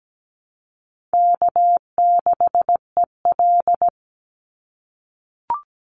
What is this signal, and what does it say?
Morse code, heard as a keyed tone, about 17 wpm, 700 Hz. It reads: K6EL
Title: NA SOTA Chaser Callsigns 17wpm